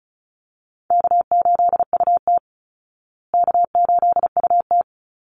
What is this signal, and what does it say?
Morse code, heard as a keyed tone, about 34 wpm, 700 Hz. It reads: K8UT K8UT
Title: Easy Call Signs Repeated 3x - 35wpm